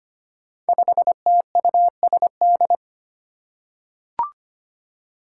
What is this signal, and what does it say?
Morse code, heard as a keyed tone, about 25 wpm, 700 Hz. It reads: 5TUSD
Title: Five Challenging Letter-Number Combinations 25wpm